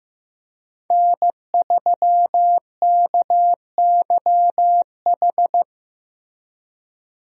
Morse code, heard as a keyed tone, about 15 wpm, 700 Hz. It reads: N3KYH
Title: USA Call Signs - 5 Characters - Repeated 2x - 15wpm